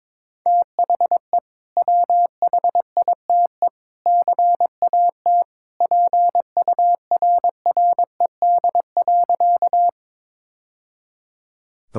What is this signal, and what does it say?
Morse code, heard as a keyed tone, about 22 wpm, 700 Hz. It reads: THE WHITE CAT PURRED.
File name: practice / 22wpm / Sentences with 2-Word Phrases - 22wpm